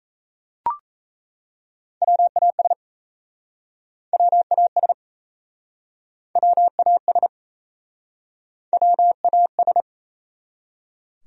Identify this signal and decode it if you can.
Morse code, heard as a keyed tone, about 39 wpm, 700 Hz. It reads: WAS WAS WAS WAS